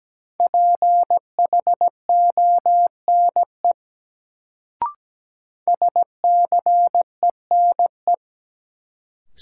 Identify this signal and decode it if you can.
Morse code, heard as a keyed tone, about 17 wpm, 700 Hz. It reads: PHONE SCENE
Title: Sentences from Top 1100 Words - New Word Drill - 17wpm